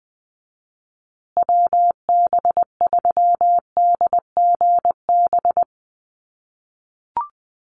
Morse code, heard as a keyed tone, about 20 wpm, 700 Hz. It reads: WB3DGB
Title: USA Call Signs Repeated 3x 20wpm